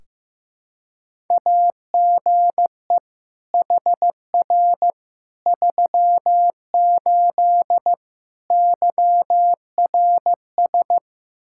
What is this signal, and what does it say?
Morse code, heard as a keyed tone, about 15 wpm, 700 Hz. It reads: AGE HR 38 YRS
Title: Practice QSO - Exchange 3 - More Info - 15wpm